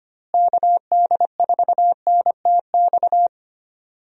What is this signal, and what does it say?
Morse code, heard as a keyed tone, about 25 wpm, 700 Hz. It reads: KD4NTX
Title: USA Call Signs - 6 Characters - Repeated 2x - 25wpm